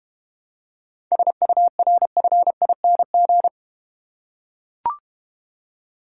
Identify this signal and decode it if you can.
Morse code, heard as a keyed tone, about 31 wpm, 700 Hz. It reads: SURFING
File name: Easy Words - Speed-Racing with Hints - 25wpm